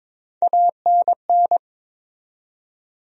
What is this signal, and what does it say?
Morse code, heard as a keyed tone, about 21 wpm, 700 Hz. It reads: ANN
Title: USA Common Names - Max 3 Letters - 22wpm